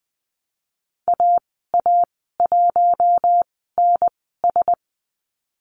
Morse code, heard as a keyed tone, about 20 wpm, spaced as for 14 wpm, 700 Hz. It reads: AA1NS
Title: 042 Lesson 8g - Number 1 - Call Signs 20wpm